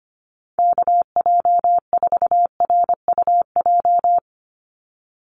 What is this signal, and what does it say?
Morse code, heard as a keyed tone, about 25 wpm, 700 Hz. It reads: KJ4RUJ
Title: USA Call Signs 25wpm